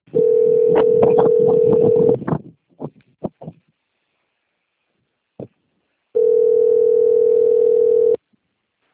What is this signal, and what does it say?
Ringback tone (North America), 440 + 480 Hz.